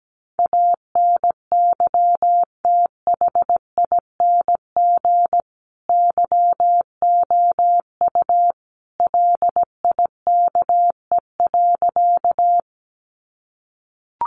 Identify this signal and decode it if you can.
Morse code, heard as a keyed tone, about 17 wpm, 700 Hz. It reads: ANYTHING YOU LIKE.
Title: Sentences from Top 1000 Words 17wpm